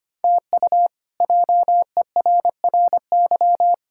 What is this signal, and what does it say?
Morse code, heard as a keyed tone, about 25 wpm, 700 Hz. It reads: TU JERRY
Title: K1USN SST QSO Practice - 25wpm